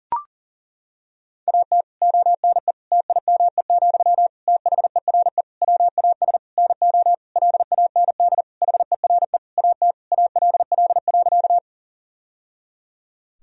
Morse code, heard as a keyed tone, about 40 wpm, 700 Hz. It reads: AT ONE TIME, THERE WAS NO LAND HERE AT ALL.
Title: Sentences from Top 1000 Words - Review Part 1 - 40wpm